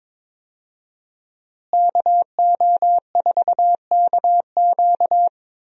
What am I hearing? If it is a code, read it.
Morse code, heard as a keyed tone, about 22 wpm, 700 Hz. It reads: KO4KQ
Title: USA Call Signs - 5 Characters - Repeated 3x - 22wpm